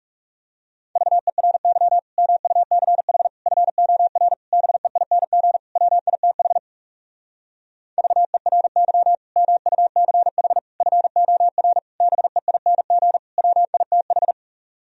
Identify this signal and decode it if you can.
Morse code, heard as a keyed tone, about 37 wpm, 700 Hz. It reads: VERY MUCH FOR BEING WITH VERY MUCH FOR BEING WITH
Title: Speed-Racing - Sets of 5 Words 30wpm